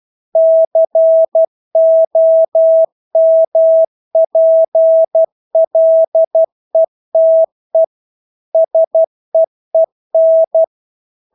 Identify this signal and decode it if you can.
Morse code, heard as a keyed tone, about 12 wpm, 650 Hz. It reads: COMPLETE SEEN
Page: Engelske ord 12 wpm